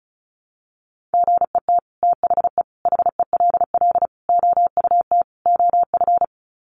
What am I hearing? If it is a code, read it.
Morse code, heard as a keyed tone, about 34 wpm, 700 Hz. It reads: GET THE HELL OUT OF
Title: Sets of 5 Words Encore2 35wpm